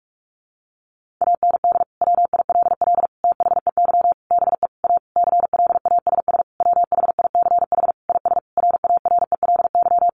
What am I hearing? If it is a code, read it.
Morse code, heard as a keyed tone, about 44 wpm, 700 Hz. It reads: AND WILL THEY BE A CLASS WHICH IS RARELY
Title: Sentences from Top 1400 Words 45wpm